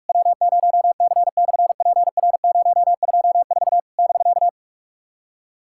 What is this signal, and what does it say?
Morse code, heard as a keyed tone, about 46 wpm, 700 Hz. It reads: W0C/PR024 <BK>
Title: SOTA QSO Practice 45wpm